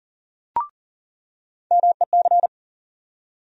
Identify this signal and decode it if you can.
Morse code, heard as a keyed tone, about 40 wpm, 700 Hz. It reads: MEC
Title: USA Call Sign Postfixes - 3 Letters 40wpm